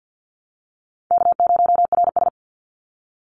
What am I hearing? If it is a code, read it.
Morse code, heard as a keyed tone, about 50 wpm, 700 Hz. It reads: K0RS
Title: NA SOTA Chaser Callsigns 50wpm